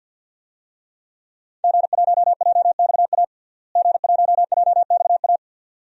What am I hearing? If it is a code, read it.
Morse code, heard as a keyed tone, about 52 wpm, 700 Hz. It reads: G1JXA G1JXA
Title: DXCC-100 Call Signs - 5 Characters - Repeated 2x - 50wpm